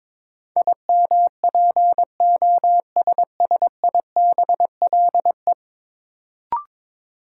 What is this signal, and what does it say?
Morse code, heard as a keyed tone, about 22 wpm, 700 Hz. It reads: IMPOSSIBLE
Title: Sentences from Top 1100 Words - New Word Drill - 22wpm